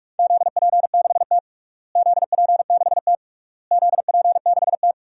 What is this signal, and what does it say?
Morse code, heard as a keyed tone, about 46 wpm, 700 Hz. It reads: ZP6T ZP6T ZP6T
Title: DXCC-100 Call Signs - 4 Characters - Repeated 3x - 45wpm